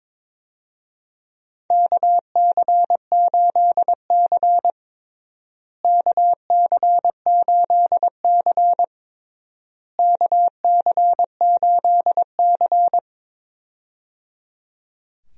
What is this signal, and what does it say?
Morse code, heard as a keyed tone, about 22 wpm, 700 Hz. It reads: KC8C KC8C KC8C
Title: USA Call Signs - 4 Characters - Repeated 3x - 22wpm